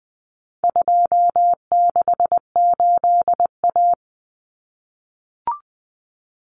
Morse code, heard as a keyed tone, about 20 wpm, 700 Hz. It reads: 268A
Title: Common Cut Numbers - 4-Digit 20wpm